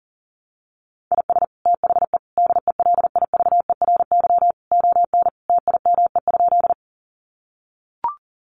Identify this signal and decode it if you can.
Morse code, heard as a keyed tone, about 40 wpm, 700 Hz. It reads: IS THE DELIVERY ON TIME?
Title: Sentences from Top 1500 Words 40wpm